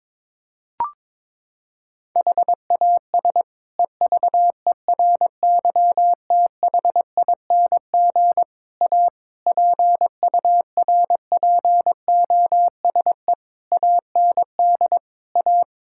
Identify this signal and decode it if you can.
Morse code, heard as a keyed tone, about 22 wpm, 700 Hz. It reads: HAS EVERYTHING A PURPOSE AND A
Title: Sentences from Top 1600 Words 22wpm